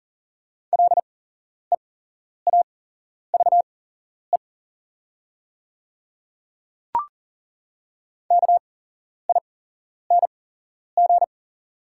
Morse code, heard as a keyed tone, about 40 wpm, spaced as for 11 wpm, 700 Hz. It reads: LEAVE KING
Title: 8x Character Spacing - Top 500 Words - 40wpm